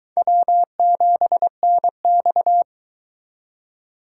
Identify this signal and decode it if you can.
Morse code, heard as a keyed tone, about 22 wpm, 700 Hz. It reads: W7NX